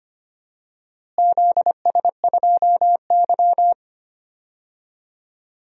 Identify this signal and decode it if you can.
Morse code, heard as a keyed tone, about 25 wpm, 700 Hz. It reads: ZS2Y
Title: DXCC-100 Call Signs - 4 Characters - Repeated 3x - 25wpm